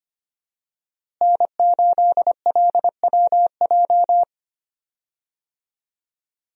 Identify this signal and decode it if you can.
Morse code, heard as a keyed tone, about 25 wpm, 700 Hz. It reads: N8LWJ
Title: USA Call Signs - 5 Characters - Repeated 3x - 25wpm